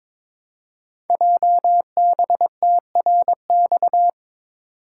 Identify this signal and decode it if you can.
Morse code, heard as a keyed tone, about 22 wpm, 700 Hz. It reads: JBTRX
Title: Code Groups - 5 Letters - 22wpm